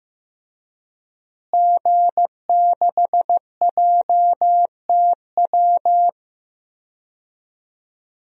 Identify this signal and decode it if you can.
Morse code, heard as a keyed tone, about 15 wpm, 700 Hz. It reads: G6JTW
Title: DXCC-100 Call Signs - 5 Characters - Repeated 3x - 15wpm